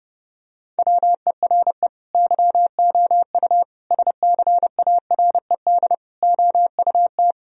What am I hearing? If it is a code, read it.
Morse code, heard as a keyed tone, about 30 wpm, 700 Hz. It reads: WERE YOU SCARED OUT
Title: Sentences from Top 1000 Words - Review Part 4 - 30wpm